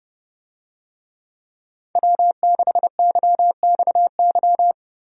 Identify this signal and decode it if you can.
Morse code, heard as a keyed tone, about 30 wpm, 700 Hz. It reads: W6YXY
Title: USA Call Signs - 5 Characters - 30wpm